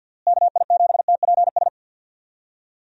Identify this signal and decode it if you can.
Morse code, heard as a keyed tone, about 50 wpm, 700 Hz. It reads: KI7TPS
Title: USA Call Signs - 6 Characters - Repeated 2x - 50wpm